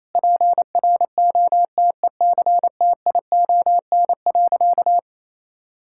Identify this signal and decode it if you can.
Morse code, heard as a keyed tone, about 28 wpm, 700 Hz. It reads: PROTECTION.
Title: Sentences from Top 1600 Words 28wpm